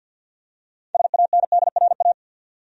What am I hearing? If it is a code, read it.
Morse code, heard as a keyed tone, about 50 wpm, 700 Hz. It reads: SANDRA
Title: USA Common Names - 50wpm